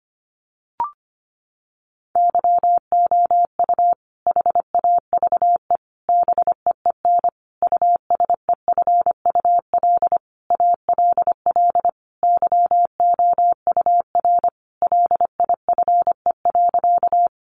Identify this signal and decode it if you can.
Morse code, heard as a keyed tone, about 25 wpm, 700 Hz. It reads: YOU HAVE BEEN USEFUL ALL YOUR LIFE.
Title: Sentences from Top 700 Words 25wpm